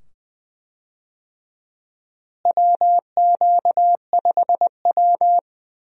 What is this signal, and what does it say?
Morse code, heard as a keyed tone, about 20 wpm, 700 Hz. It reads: WQ5W
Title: USA Call Signs - 4 Characters - Repeated 2x - 20wpm